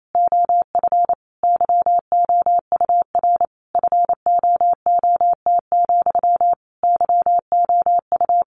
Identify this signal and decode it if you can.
Morse code, heard as a keyed tone, about 27 wpm, 700 Hz. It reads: OF YOUR FOOT, YOU
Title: A Sound Of Thunder 28wpm